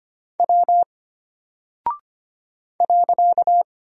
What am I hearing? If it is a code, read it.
Morse code, heard as a keyed tone, about 25 wpm, 700 Hz. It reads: W .
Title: Single Character (Letter:Number:Punctuation:Prosign) Alt 25wpm